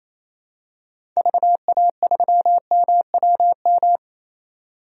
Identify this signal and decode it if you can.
Morse code, heard as a keyed tone, about 28 wpm, 700 Hz. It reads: VA3MWM
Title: DXCC-100 Call Signs - 6 Characters - 28wpm